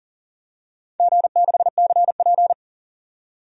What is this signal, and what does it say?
Morse code, heard as a keyed tone, about 40 wpm, 700 Hz. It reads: G6CP
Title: DXCC-100 Call Signs - 4 Characters - 40wpm